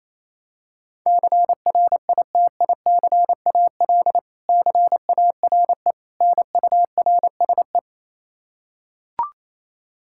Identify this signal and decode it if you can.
Morse code, heard as a keyed tone, about 28 wpm, 700 Hz. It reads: CRITICAL CARE NURSE
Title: Practice QSO - Exchange 3 - More Info - 28wpm